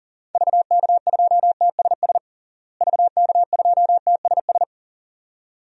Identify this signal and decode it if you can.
Morse code, heard as a keyed tone, about 40 wpm, 700 Hz. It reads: VK2TSS VK2TSS
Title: DXCC-100 Call Signs - 6 Characters - Repeated 2x - 40wpm